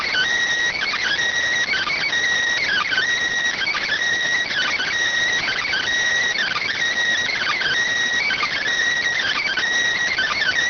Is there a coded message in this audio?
Yes: Emergency Alert System (SAME) tones (header bursts) from 0.8 to 9.6 s, heard as harsh buzzing bursts.